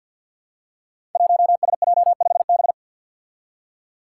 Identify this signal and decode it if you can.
Morse code, heard as a keyed tone, about 48 wpm, 700 Hz. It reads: 1SJ5B
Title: Five Challenging Letter-Number Combinations 50wpm